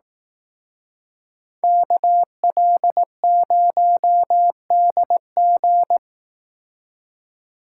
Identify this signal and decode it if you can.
Morse code, heard as a keyed tone, about 18 wpm, 700 Hz. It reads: KL0DG
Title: Speed Racing - USA 2x2 Call Signs 15wpm